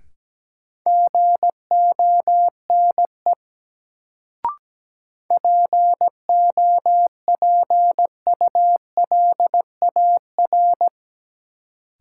Morse code, heard as a keyed tone, about 17 wpm, 700 Hz. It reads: GONE POPULAR
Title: Sentences from Top 1200 Words - New Word Drill Alt - 17wpm